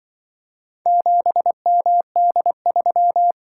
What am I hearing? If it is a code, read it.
Morse code, heard as a keyed tone, about 25 wpm, 700 Hz. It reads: 7MD3